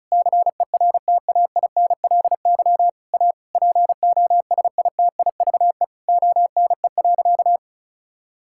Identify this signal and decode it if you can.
Morse code, heard as a keyed tone, about 34 wpm, 700 Hz. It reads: CERTAINLY A POSITIVE ONE.
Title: Sentences from Top 1000 Words - Review Part 6 - 35wpm